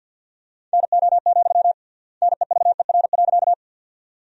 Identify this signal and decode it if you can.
Morse code, heard as a keyed tone, about 50 wpm, 700 Hz. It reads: NO, NEVER.
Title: 3x Word-Spacing - Sentences from Top 300 Words - 50wpm